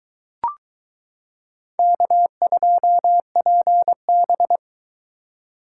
Morse code, heard as a keyed tone, about 22 wpm, 700 Hz. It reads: K2PB